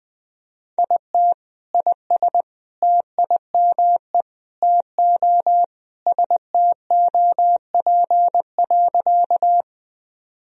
Morse code, heard as a keyed tone, about 21 wpm, 700 Hz. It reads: IT IS TIME TO STOP.
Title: Sentences from Top 1000 Words - Review Part 1 - 20wpm